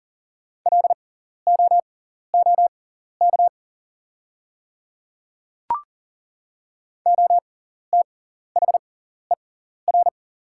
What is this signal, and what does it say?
Morse code, heard as a keyed tone, about 40 wpm, spaced as for 14 wpm, 690 Hz. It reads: LOOK OTHER
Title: 6x Character Spacing - Top 300 Words - 40wpm